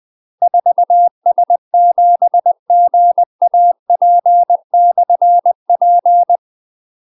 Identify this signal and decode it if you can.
Morse code, heard as a keyed tone, about 20 wpm, 700 Hz. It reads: 4S7GAP/P
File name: UK MASTER QSOs TRAINING / 20WPM / UK MASTER QSOs 0050